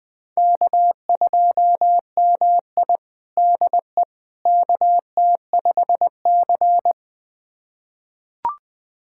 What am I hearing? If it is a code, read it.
Morse code, heard as a keyed tone, about 20 wpm, 700 Hz. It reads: K2MI DE KT5C
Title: USA Callsign DE Callsign - 4 Characters - 20wpm